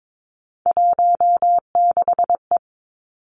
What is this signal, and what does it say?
Morse code, heard as a keyed tone, about 22 wpm, 700 Hz. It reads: 16E
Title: Three Challenging Letter-Number Combinations 22wpm